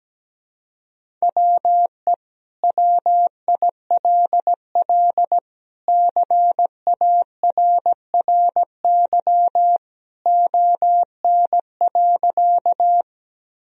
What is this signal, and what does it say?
Morse code, heard as a keyed tone, about 17 wpm, 700 Hz. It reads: WE WILL CARRY ON.
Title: Sentences from Top 1000 Words - Review Part 1 - 17wpm